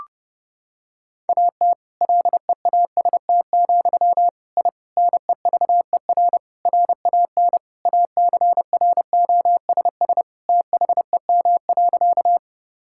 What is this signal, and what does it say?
Morse code, heard as a keyed tone, about 32 wpm, 700 Hz. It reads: AT LEAST, I NEVER RAN ACROSS THEM.
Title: Sentences from Top 1000 Words - Review Part 5 - 30wpm